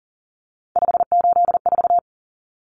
Morse code, heard as a keyed tone, about 40 wpm, 700 Hz. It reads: 584
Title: Numbers - 3-Digit 40wpm